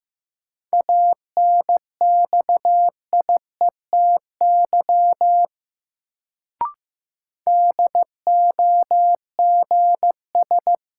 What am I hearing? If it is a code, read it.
Morse code, heard as a keyed tone, about 15 wpm, 700 Hz. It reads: ANXIETY DOGS
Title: Sentences from Top 1600 Words - New Word Drill Alt - 15wpm